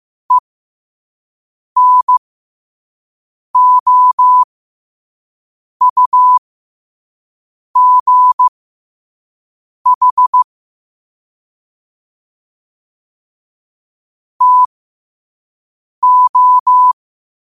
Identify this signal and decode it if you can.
Morse code, heard as a keyed tone, about 15 wpm, spaced as for 5 wpm, 1000 Hz. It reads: ENOUGH TO
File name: qotdfiles / 5wpm